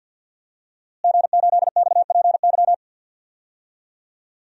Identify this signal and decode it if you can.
Morse code, heard as a keyed tone, about 47 wpm, 700 Hz. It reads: G8XPY
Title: DXCC-100 Call Signs - 5 Characters - Repeated 3x - 50wpm